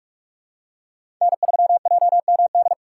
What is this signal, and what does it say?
Morse code, heard as a keyed tone, about 43 wpm, 690 Hz. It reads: N3JMD
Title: USA Call Signs - 5 Characters - 4x Character Spacing - 45wpm